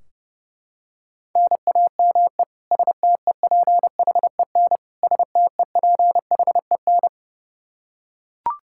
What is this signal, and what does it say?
Morse code, heard as a keyed tone, about 32 wpm, 700 Hz. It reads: NAME STEPHEN STEPHEN
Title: Practice QSO - Exchange 1 - Beginning - 30wpm